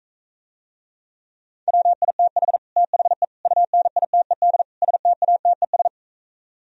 Morse code, heard as a keyed tone, about 45 wpm, 700 Hz. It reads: WITH THE UNITED STATES